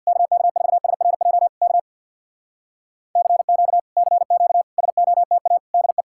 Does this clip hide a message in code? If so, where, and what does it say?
Yes: Morse code from 0.1 to 6.0 s, heard as a keyed tone, about 64 wpm, 700 Hz. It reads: KK4SRJ K CQ CQ SOTA DE